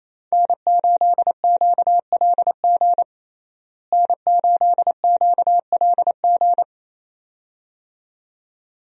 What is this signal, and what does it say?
Morse code, heard as a keyed tone, about 27 wpm, 700 Hz. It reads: N8QLG N8QLG
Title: USA Call Signs - 5 Characters - Repeated 2x - 28wpm